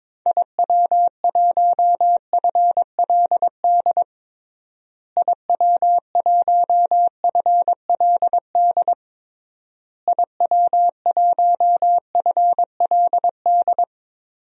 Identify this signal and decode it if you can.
Morse code, heard as a keyed tone, about 22 wpm, 700 Hz. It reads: IW1FLD IW1FLD IW1FLD
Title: DXCC-100 Call Signs - 6 Characters - Repeated 3x - 22wpm